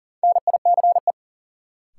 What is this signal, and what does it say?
Morse code, heard as a keyed tone, about 40 wpm, 700 Hz. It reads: NICE
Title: Sentences from Top 1100 Words - New Word Drill - 40wpm